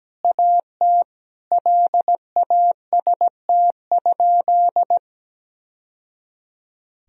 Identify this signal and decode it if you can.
Morse code, heard as a keyed tone, about 17 wpm, 700 Hz. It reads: AT LAST?
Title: Sentences from Top 1000 Words - Review Part 4 - 17wpm